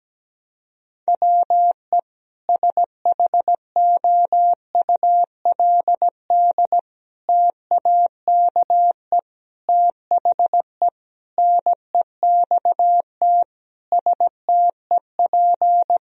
Morse code, heard as a keyed tone, about 17 wpm, 700 Hz. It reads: WE SHOULD TAKE THE NEXT STEP
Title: Sentences from Top 1000 Words - Review Part 2 - 17wpm